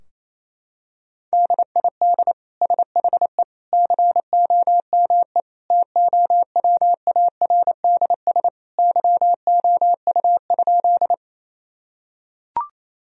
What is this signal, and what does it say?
Morse code, heard as a keyed tone, about 27 wpm, 700 Hz. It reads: DID SHE COME TOWARDS YOU?
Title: Sentences from Top 1000 Words - Review Part 2 - 28wpm